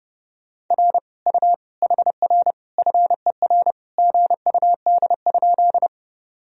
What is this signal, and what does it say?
Morse code, heard as a keyed tone, about 30 wpm, 700 Hz. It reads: R U HR FER GUD?
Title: Sentences from Top 200 Words and Abbreviations 30wpm